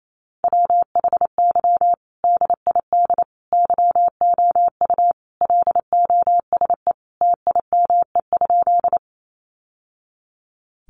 Morse code, heard as a keyed tone, about 28 wpm, 700 Hz. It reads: WHY DID YOU LOSE TIME?
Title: Sentences from Top 1100 Words 28wpm